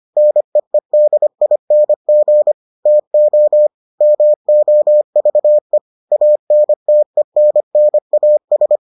Morse code, heard as a keyed tone, about 25 wpm, 600 Hz. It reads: NEEDING TO MOVE ANTENNAS